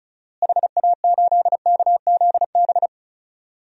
Morse code, heard as a keyed tone, about 34 wpm, 700 Hz. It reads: HA8KZB
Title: DXCC-100 Call Signs - 6 Characters - Repeated 3x - 35wpm